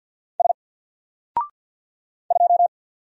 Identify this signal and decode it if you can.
Morse code, heard as a keyed tone, about 48 wpm, 700 Hz. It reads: S 2
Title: Single Letter-Number Alt 50wpm